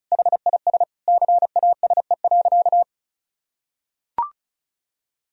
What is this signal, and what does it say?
Morse code, heard as a keyed tone, about 34 wpm, 700 Hz. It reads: HIS CASE.
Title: Sentences from Top 1000 Words - Review Part 2 - 35wpm